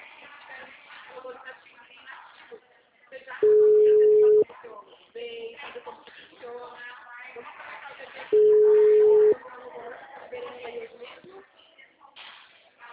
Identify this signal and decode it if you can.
Ringback tone (Europe, 425 Hz), with speech.
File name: out- -2006- - - .6